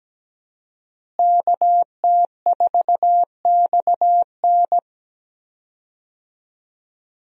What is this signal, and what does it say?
Morse code, heard as a keyed tone, about 17 wpm, 700 Hz. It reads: KT4XN
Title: USA Call Signs - 5 Characters - Repeated 3x - 17wpm